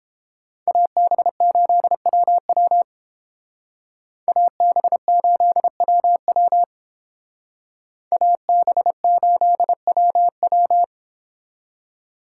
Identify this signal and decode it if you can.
Morse code, heard as a keyed tone, about 28 wpm, 700 Hz. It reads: AB8WW AB8WW AB8WW